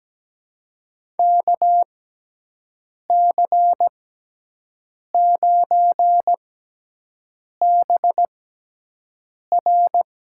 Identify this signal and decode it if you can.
Morse code, heard as a keyed tone, about 17 wpm, spaced as for 6 wpm, 700 Hz. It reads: KC9BR
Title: USA Call Signs - 6x Character Spacing - 17wpm